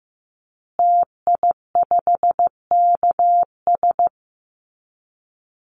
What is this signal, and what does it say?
Morse code, heard as a keyed tone, about 15 wpm, 700 Hz. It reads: TI5KS
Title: Call Signs Repeated 2x - 15wpm